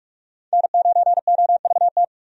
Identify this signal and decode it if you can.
Morse code, heard as a keyed tone, about 41 wpm, 700 Hz. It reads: N9OVT
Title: USA Call Signs - 5 Characters - 45wpm